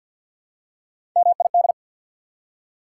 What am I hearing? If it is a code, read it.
Morse code, heard as a keyed tone, about 48 wpm, 700 Hz. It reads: MID
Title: Prefixes and Suffixes 50wpm